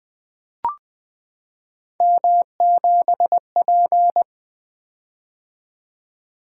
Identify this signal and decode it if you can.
Morse code, heard as a keyed tone, about 20 wpm, 700 Hz. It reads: M7P
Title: Three Letter-Number Combinations - 20wpm